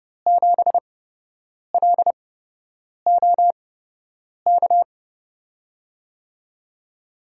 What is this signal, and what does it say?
Morse code, heard as a keyed tone, about 32 wpm, spaced as for 8 wpm, 700 Hz. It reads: 7LOK
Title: USA Call Signs - 6 Characters - 8x Character Spacing - 30wpm